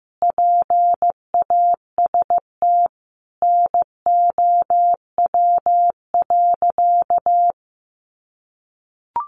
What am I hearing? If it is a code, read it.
Morse code, heard as a keyed tone, about 15 wpm, 700 Hz. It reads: PAST NOW.
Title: Sentences from Top 600 Words 15wpm